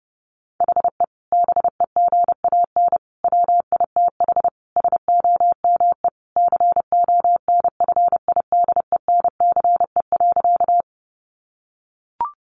Morse code, heard as a keyed tone, about 30 wpm, 700 Hz. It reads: HE BEGAN WITH SOME CONFIDENCE.
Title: Sentences from Top 1500 Words 30wpm